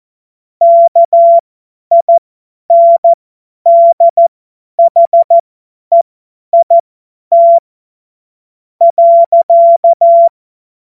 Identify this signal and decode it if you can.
Morse code, heard as a keyed tone, about 14 wpm, spaced as for 9 wpm, 680 Hz. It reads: KINDHEIT.
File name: Morsekurs / Witze / Witz 18. April 2023 50 F70